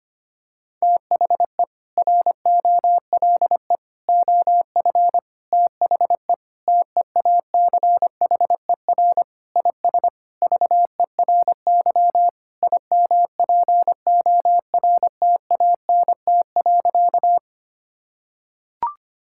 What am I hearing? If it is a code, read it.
Morse code, heard as a keyed tone, about 25 wpm, 700 Hz. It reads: THE ROLE OF THE TEACHER IS VERY IMPORTANT.
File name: practice / 25wpm / Sentences with 4-Word Phrases - 25wpm